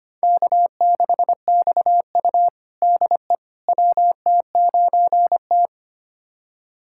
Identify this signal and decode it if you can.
Morse code, heard as a keyed tone, about 25 wpm, 700 Hz. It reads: K6XU DE WT9T
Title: USA Callsign DE Callsign - 4 Characters - 25wpm